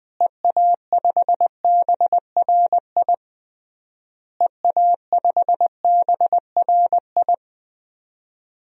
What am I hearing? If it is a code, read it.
Morse code, heard as a keyed tone, about 19 wpm, 700 Hz. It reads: EA5BRI EA5BRI
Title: DXCC-100 Call Signs - 6 Characters - Repeated 3x - 20wpm